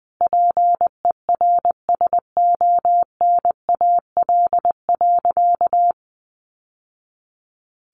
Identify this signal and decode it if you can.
Morse code, heard as a keyed tone, about 20 wpm, 700 Hz. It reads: PERSONAL.
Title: Sentences from Top 700 Words 20wpm